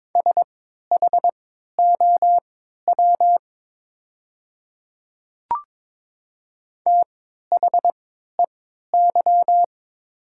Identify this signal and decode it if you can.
Morse code, heard as a keyed tone, about 22 wpm, spaced as for 12 wpm, 700 Hz. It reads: SHOW THEY
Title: 3x Character Spacing - Top 200 Words - 22wpm